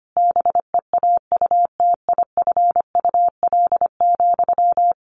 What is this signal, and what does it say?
Morse code, heard as a keyed tone, about 25 wpm, 700 Hz. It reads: BEAUTIFUL,
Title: Sentences from Top 1200 Words 25wpm